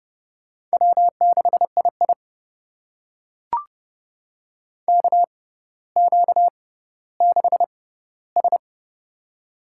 Morse code, heard as a keyed tone, about 30 wpm, spaced as for 15 wpm, 700 Hz. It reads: W6II KQ6S
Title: USA Call Signs - 4 Characters - 6x Character Spacing - 30wpm